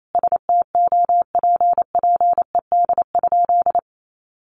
Morse code, heard as a keyed tone, about 28 wpm, 700 Hz. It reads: STOPPED?
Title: A Sound Of Thunder 28wpm